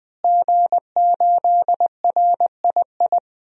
Morse code, heard as a keyed tone, about 20 wpm, 700 Hz. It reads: G8RII
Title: DXCC-100 Call Signs - 5 Characters - Repeated 3x - 20wpm